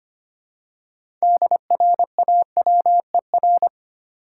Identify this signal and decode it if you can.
Morse code, heard as a keyed tone, about 25 wpm, 700 Hz. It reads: DRAWER
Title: Six Letter Words - 25wpm